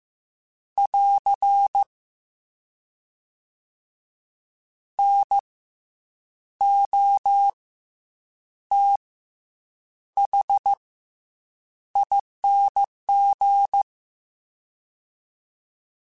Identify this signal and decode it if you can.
Morse code, heard as a keyed tone, about 15 wpm, 800 Hz. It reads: <AR> N O T H ING
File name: Special INGs / Special INGs-mp3s / SlowToFaster ing @13